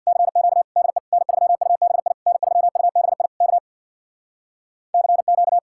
Morse code, heard as a keyed tone, about 64 wpm, 700 Hz. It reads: CQ DE N3UBI N3UBI K CQ